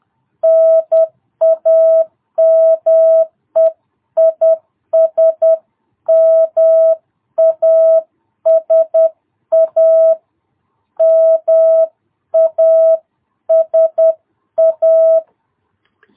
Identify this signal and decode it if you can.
Morse code, heard as a keyed tone, about 10 wpm, 660 Hz. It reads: NAME IS MASA MASA
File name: jh8cbh / 2022cw